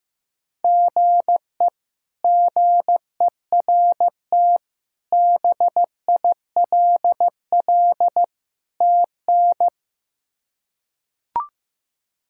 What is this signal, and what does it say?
Morse code, heard as a keyed tone, about 15 wpm, 700 Hz. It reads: GE GERT BILL TN
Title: K1USN SST QSO Practice - 15wpm